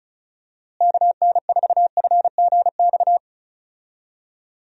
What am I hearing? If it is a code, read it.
Morse code, heard as a keyed tone, about 36 wpm, 700 Hz. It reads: KN4FGX
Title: USA Call Signs - 6 Characters - Repeated 2x - 35wpm